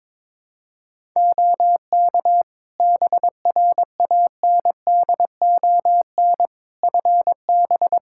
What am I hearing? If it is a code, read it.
Morse code, heard as a keyed tone, about 22 wpm, 700 Hz. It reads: OK BRANDON FB
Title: Practice QSO - Exchange 2 - Info - 22wpm